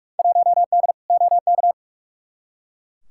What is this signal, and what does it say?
Morse code, heard as a keyed tone, about 43 wpm, 700 Hz. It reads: 1D OK
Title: Field Day Exchanges - 45wpm